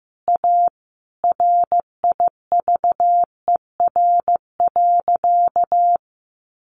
Morse code, heard as a keyed tone, about 15 wpm, 700 Hz. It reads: A RIVER.
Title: Sentences from Top 1100 Words 15wpm